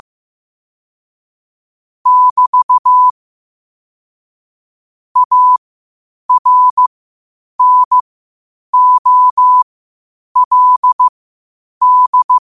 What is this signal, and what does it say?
Morse code, heard as a keyed tone, about 15 wpm, spaced as for 8 wpm, 1000 Hz. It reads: <BT> ARNOLD
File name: qotdfiles / 8wpm